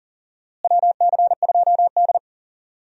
Morse code, heard as a keyed tone, about 40 wpm, 700 Hz. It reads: WC2D
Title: USA Call Signs - 4 Characters - Repeated 3x - 40wpm